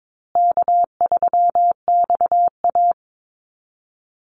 Morse code, heard as a keyed tone, about 22 wpm, 700 Hz. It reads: K3XA
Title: Call Signs Repeated 3x - 22wpm